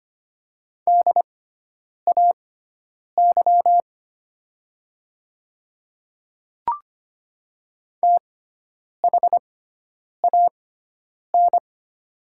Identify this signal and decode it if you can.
Morse code, heard as a keyed tone, about 25 wpm, spaced as for 9 wpm, 700 Hz. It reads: DAY THAN
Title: 6x Character Spacing - Top 100 Words - 25wpm